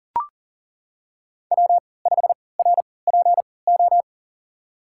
Morse code, heard as a keyed tone, about 40 wpm, spaced as for 23 wpm, 700 Hz. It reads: W5RPO
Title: USA Call Signs - 5 Characters - 3x Character Spacing - 40wpm